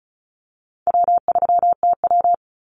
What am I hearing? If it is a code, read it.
Morse code, heard as a keyed tone, about 34 wpm, 700 Hz. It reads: W3TW
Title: Call Signs - 35wpm